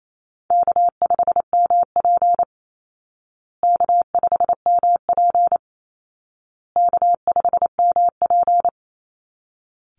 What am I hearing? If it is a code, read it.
Morse code, heard as a keyed tone, about 28 wpm, 700 Hz. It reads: K5MP K5MP K5MP
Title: Call Signs Repeated 2x - 28wpm